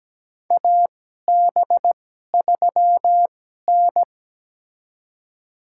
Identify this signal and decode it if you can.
Morse code, heard as a keyed tone, about 17 wpm, spaced as for 12 wpm, 700 Hz. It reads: AB3N
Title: USA Call Signs - 4 Characters - 2x Character Spacing - 17wpm